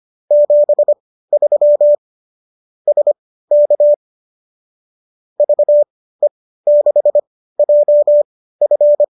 Morse code, heard as a keyed tone, about 25 wpm, spaced as for 15 wpm, 590 Hz. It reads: 73 SK VE6JF
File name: qso203 15